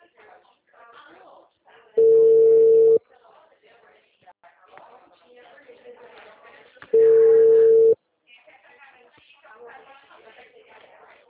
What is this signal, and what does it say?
Ringback tone (China, 450 Hz), with speech.
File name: out- -2010- - - .20430